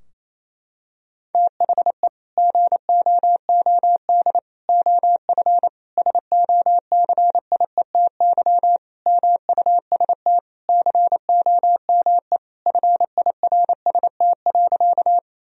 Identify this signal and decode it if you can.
Morse code, heard as a keyed tone, about 28 wpm, 700 Hz. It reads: THE GOOD OF SOCIETY MUST COME FIRST.
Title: Sentences from Top 1000 Words - Review Part 4 - 28wpm